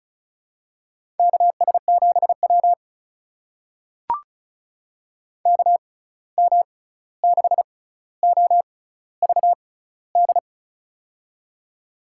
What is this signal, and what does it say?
Morse code, heard as a keyed tone, about 36 wpm, spaced as for 16 wpm, 700 Hz. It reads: KS7W KM6OVD
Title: USA Call Signs - 6x Character Spacing - 35wpm